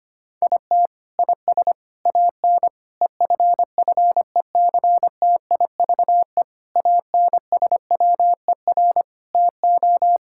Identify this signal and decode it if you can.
Morse code, heard as a keyed tone, about 25 wpm, 700 Hz. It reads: IT IS AN EFFECTIVE ANSWER TO
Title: Sentences from Top 1000 Words - Review Part 6 - 25wpm